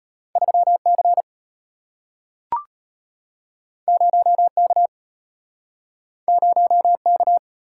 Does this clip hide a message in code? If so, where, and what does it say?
Yes: Morse code from 0.4 to 7.4 s, heard as a keyed tone, about 33 wpm, 690 Hz. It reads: 3C 0K 0K